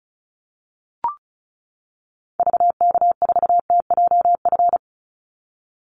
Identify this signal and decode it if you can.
Morse code, heard as a keyed tone, about 36 wpm, 700 Hz. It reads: VK4TJF
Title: Call Signs - 35wpm